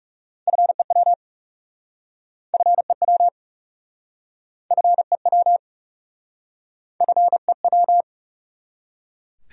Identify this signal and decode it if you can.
Morse code, heard as a keyed tone, about 38 wpm, 700 Hz. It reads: FEW FEW FEW FEW